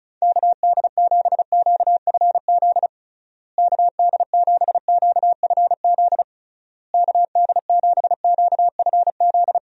Morse code, heard as a keyed tone, about 36 wpm, 700 Hz. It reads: KD7QFZ KD7QFZ KD7QFZ
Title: USA Call Signs - 6 Characters - Repeated 3x - 35wpm